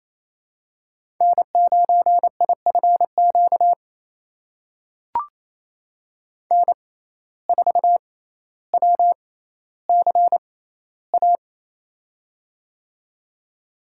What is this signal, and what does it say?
Morse code, heard as a keyed tone, about 28 wpm, spaced as for 14 wpm, 700 Hz. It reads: N9IFQ N4WCA
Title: USA Call Signs - 5 Characters - 6x Character Spacing - 28wpm